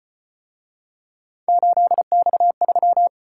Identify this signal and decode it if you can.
Morse code, heard as a keyed tone, about 33 wpm, 700 Hz. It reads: 8X3